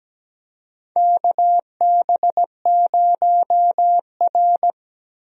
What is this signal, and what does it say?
Morse code, heard as a keyed tone, about 17 wpm, 700 Hz. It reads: KB0R
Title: USA Call Signs - 4 Characters - Repeated 3x - 17wpm